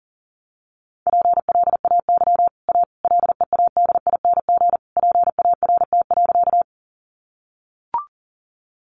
Morse code, heard as a keyed tone, about 40 wpm, 700 Hz. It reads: PLAY A LEADING PART.
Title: Sentences from Top 1100 Words 40wpm